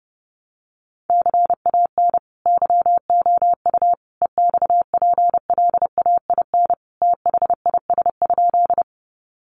Morse code, heard as a keyed tone, about 28 wpm, 700 Hz. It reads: CAN YOU EXPLAIN THIS?
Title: Sentences from Top 1100 Words 30wpm